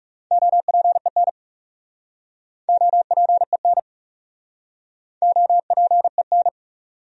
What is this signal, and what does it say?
Morse code, heard as a keyed tone, about 40 wpm, 700 Hz. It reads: OPEN OPEN OPEN